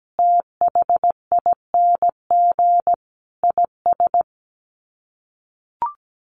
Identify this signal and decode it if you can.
Morse code, heard as a keyed tone, about 17 wpm, 700 Hz. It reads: THING IS
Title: Sets of 5 Words Encore 17wpm